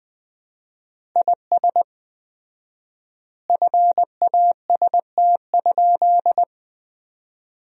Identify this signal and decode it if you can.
Morse code, heard as a keyed tone, about 19 wpm, 700 Hz. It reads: IS FAST?
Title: 4x Word-Spacing - Sentences from Top 400 Words - 20wpm